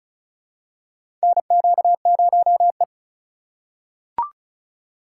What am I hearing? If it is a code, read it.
Morse code, heard as a keyed tone, about 34 wpm, 700 Hz. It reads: NQ0E
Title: USA Call Signs - 4 Characters - 2x Character Spacing - 35wpm